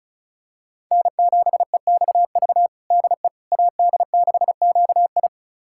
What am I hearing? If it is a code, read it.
Morse code, heard as a keyed tone, about 34 wpm, 690 Hz. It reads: N7EXV DE AD6QI
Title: USA Callsign DE Callsign - 5 Characters - 35wpm